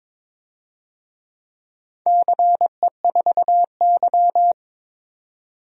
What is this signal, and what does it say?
Morse code, heard as a keyed tone, about 22 wpm, 690 Hz. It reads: CE4Y
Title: DXCC-100 Call Signs - 4 Characters - 22wpm